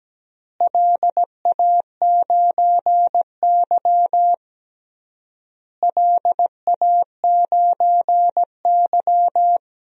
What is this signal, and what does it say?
Morse code, heard as a keyed tone, about 17 wpm, 700 Hz. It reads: LA9Y LA9Y
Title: DXCC-100 Call Signs - 4 Characters - Repeated 3x - 17wpm